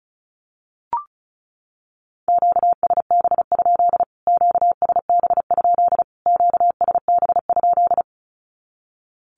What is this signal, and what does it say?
Morse code, heard as a keyed tone, about 34 wpm, 700 Hz. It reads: QSB? QSB? QSB?
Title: Q-Codes Repeated 3x 35wpm